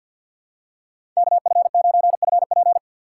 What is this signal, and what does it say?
Morse code, heard as a keyed tone, about 51 wpm, 700 Hz. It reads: KF9FP
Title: USA Call Signs - 4x Character Spacing - 50wpm